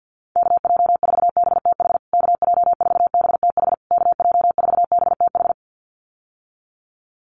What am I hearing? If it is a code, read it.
Morse code, heard as a keyed tone, about 48 wpm, 700 Hz. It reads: KJ4BTH KJ4BTH KJ4BTH
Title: USA Call Signs Repeated 3x 50wpm